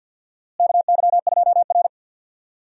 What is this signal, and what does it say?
Morse code, heard as a keyed tone, about 50 wpm, 700 Hz. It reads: KY2R
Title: USA Call Signs - 4 Characters - Repeated 3x - 50wpm